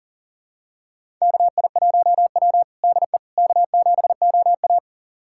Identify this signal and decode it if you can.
Morse code, heard as a keyed tone, about 37 wpm, 700 Hz. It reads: KI1W DE K7OA
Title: USA Callsign DE Callsign - 4 Characters - 40wpm